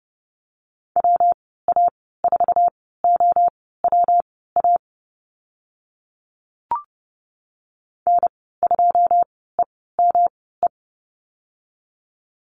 Morse code, heard as a keyed tone, about 28 wpm, spaced as for 17 wpm, 700 Hz. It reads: WA4OWA N2EME
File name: Lesson 19g - Letter w - Call Signs 30wpm